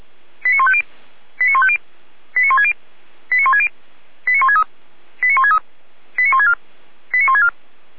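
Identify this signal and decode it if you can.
Five-tone selective call (ZVEI), heard as quick runs of five tones, tones of about 70 ms. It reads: 89150 89150 89160 89160 89163 89163 89164 89164
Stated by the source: ZVEI five-tone call 89121 received on an SDR.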